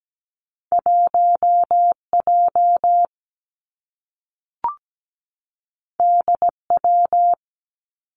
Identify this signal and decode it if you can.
Morse code, heard as a keyed tone, about 17 wpm, 700 Hz. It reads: 1J DW
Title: Two Challenging Letter-Number Combinations 17wpm